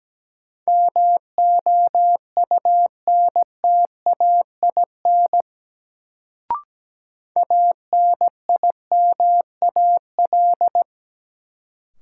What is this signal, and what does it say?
Morse code, heard as a keyed tone, about 17 wpm, 700 Hz. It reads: MOUNTAIN ANIMAL
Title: Sentences from Top 1000 Words - Review Part 1 - New Word Drill - 17wpm